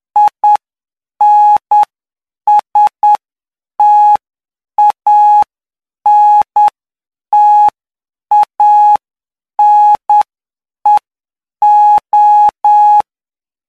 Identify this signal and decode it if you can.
Morse code, heard as a keyed tone, about 9 wpm, spaced as for 7 wpm, 840 Hz. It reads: INSTANTANEO